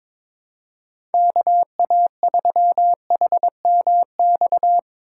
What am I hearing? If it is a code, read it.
Morse code, heard as a keyed tone, about 22 wpm, 700 Hz. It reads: KA3HMX
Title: USA Call Signs - 8x Character Spacing - 22wpm